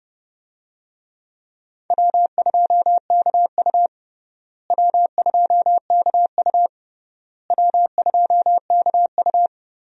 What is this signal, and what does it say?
Morse code, heard as a keyed tone, about 30 wpm, 700 Hz. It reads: W2KU W2KU W2KU
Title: USA Call Signs - 4 Characters - Repeated 3x - 30wpm